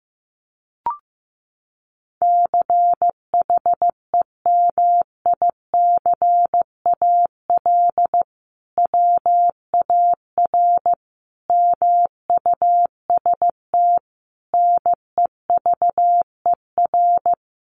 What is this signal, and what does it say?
Morse code, heard as a keyed tone, about 15 wpm, 700 Hz. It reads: CHEMICAL WAR MUST NEVER
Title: Sentences from Top 1200 Words 15wpm